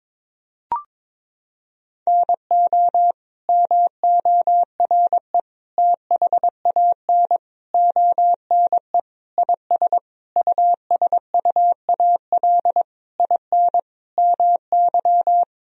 Morse code, heard as a keyed tone, about 22 wpm, 700 Hz. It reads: NO MORE THAN ONE IS USUAL IN MY
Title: Sentences from Top 1000 Words - Review Part 2 - 22wpm